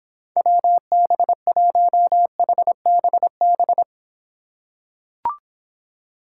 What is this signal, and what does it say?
Morse code, heard as a keyed tone, about 26 wpm, 700 Hz. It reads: WB1HBB